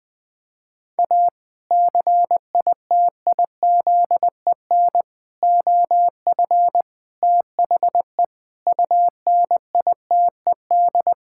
Morse code, heard as a keyed tone, about 20 wpm, 700 Hz. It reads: A CITIZEN OF THE UNITED
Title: Sentences with 3-Word Phrases - 20wpm